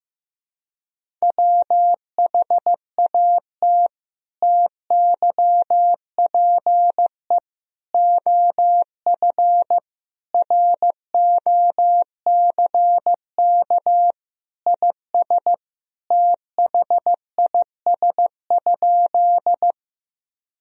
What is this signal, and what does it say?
Morse code, heard as a keyed tone, about 15 wpm, 700 Hz. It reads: WHAT TYPE OF ROCK IS THIS?
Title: Sentences from Top 1000 Words - Review Part 3 - 15wpm